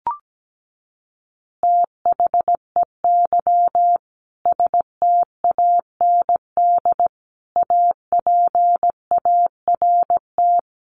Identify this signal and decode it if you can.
Morse code, heard as a keyed tone, about 17 wpm, 700 Hz. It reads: THEY STAND APART
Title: Easy Phrases 17wpm